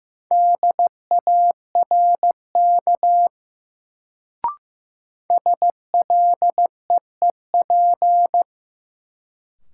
Morse code, heard as a keyed tone, about 15 wpm, 700 Hz. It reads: DARK SLEEP
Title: Sentences from Top 500 Words - New Word Drill - 15wpm